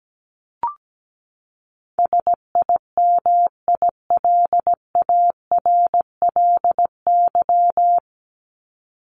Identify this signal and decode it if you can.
Morse code, heard as a keyed tone, about 17 wpm, 700 Hz. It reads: SIMILARLY
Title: LY Suffix - Top 100 Words with Base Words 17wpm